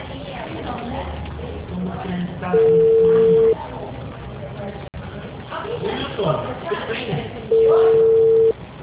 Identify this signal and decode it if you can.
Ringback tone (China, 450 Hz), with speech.